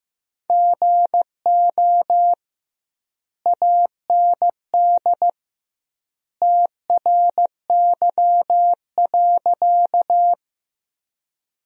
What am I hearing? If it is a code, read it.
Morse code, heard as a keyed tone, about 15 wpm, 700 Hz. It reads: GO AND TRY.
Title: 2x Word-Spacing - Sentences from Top 200 Words - 15wpm